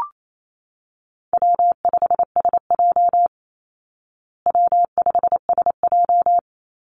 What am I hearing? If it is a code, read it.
Morse code, heard as a keyed tone, about 28 wpm, 700 Hz. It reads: W5SJ W5SJ
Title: Easy Call Signs Repeated 2x - 28wpm